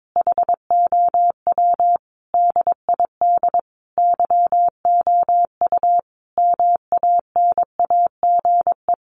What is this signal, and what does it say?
Morse code, heard as a keyed tone, about 22 wpm, 700 Hz. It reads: HOW DID YOU MANAGE
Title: Sentences from Top 1400 Words 22wpm